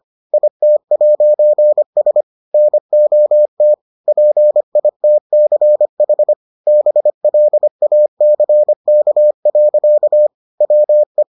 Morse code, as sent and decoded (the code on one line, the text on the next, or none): .. - .----. ... / -. --- - / .--. .. - -.-. .... / -... .-.. .- -.-. -.- .-.-.- / .-- .
IT'S NOT PITCH BLACK. WE